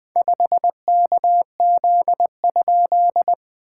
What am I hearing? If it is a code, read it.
Morse code, heard as a keyed tone, about 20 wpm, 690 Hz. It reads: 5KZ?
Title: SOTA QSO Practice 20wpm